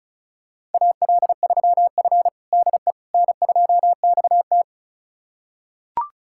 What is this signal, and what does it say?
Morse code, heard as a keyed tone, about 36 wpm, 700 Hz. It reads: AL3F DE N2XT
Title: USA Callsign DE Callsign - 4 Characters - 35wpm